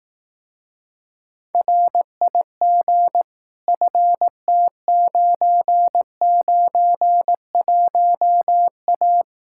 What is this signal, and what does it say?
Morse code, heard as a keyed tone, about 18 wpm, 700 Hz. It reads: RIG FT991A
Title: Speed-Racing Common QSO Phrases Encore - 15wpm